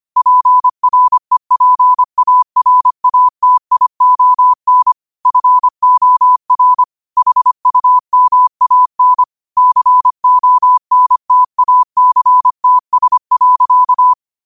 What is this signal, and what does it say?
Morse code, heard as a keyed tone, about 25 wpm, 1000 Hz. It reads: PREPARATION FOR HUMAN CONTACTS.